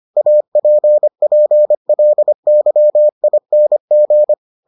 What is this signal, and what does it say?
Morse code, heard as a keyed tone, about 25 wpm, 600 Hz. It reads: APPLYING